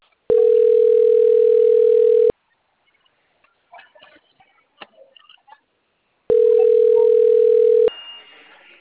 Ringback tone (North America), 440 + 480 Hz.
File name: out- - - - - .6830